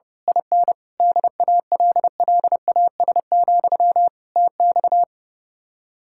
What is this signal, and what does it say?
Morse code, heard as a keyed tone, about 28 wpm, 700 Hz. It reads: IN DALLAS, TX
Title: Speed-Racing Common QSO Phrases - 30wpm